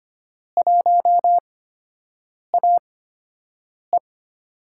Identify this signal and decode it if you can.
Morse code, heard as a keyed tone, about 25 wpm, spaced as for 7 wpm, 700 Hz. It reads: 1AE
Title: USA Call Signs - 4 Characters - 8x Character Spacing - 25wpm